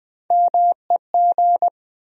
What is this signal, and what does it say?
Morse code, heard as a keyed tone, about 20 wpm, 700 Hz. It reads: MEG
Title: USA Common Names - Max 3 Letters - 20wpm